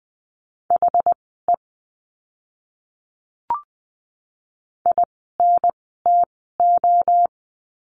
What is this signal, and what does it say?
Morse code, heard as a keyed tone, about 20 wpm, spaced as for 15 wpm, 700 Hz. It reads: HE INTO
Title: 146 Lesson 24f - Letter y - Words 20wpm